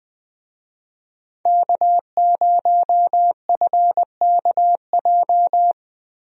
Morse code, heard as a keyed tone, about 20 wpm, 700 Hz. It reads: K0FKJ
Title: USA Call Signs - 5 Characters - Repeated 2x - 20wpm